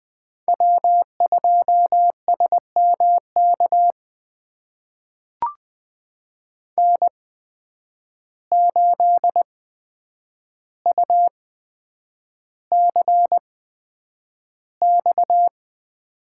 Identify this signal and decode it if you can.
Morse code, heard as a keyed tone, about 20 wpm, spaced as for 9 wpm, 700 Hz. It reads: W2SMK N8UCX
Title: USA Call Signs - 5 Characters - 8x Character Spacing - 20wpm